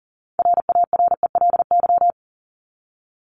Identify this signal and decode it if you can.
Morse code, heard as a keyed tone, about 40 wpm, 700 Hz. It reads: RARELY
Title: Top 1251-1500 Words 40wpm